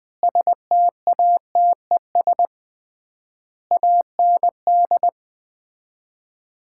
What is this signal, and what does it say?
Morse code, heard as a keyed tone, about 20 wpm, 700 Hz. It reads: STATES AND
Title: 3x Word-Spacing - Sets of 4 Words - 20wpm